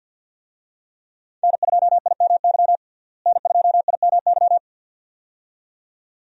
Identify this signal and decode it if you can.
Morse code, heard as a keyed tone, about 52 wpm, 700 Hz. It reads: N2IMY N2IMY
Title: USA Call Signs - 5 Characters - Repeated 2x - 50wpm